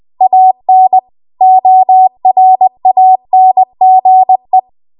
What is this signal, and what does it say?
Morse code, heard as a keyed tone, about 20 wpm, 750 Hz. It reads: AN ORANGE
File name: Archive / 20 WPM / 20WPM